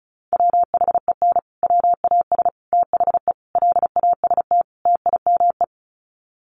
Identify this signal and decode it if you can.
Morse code, heard as a keyed tone, about 34 wpm, 700 Hz. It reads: WHEN WAS THE LAST TIME
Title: Sets of 5 Words Encore 35wpm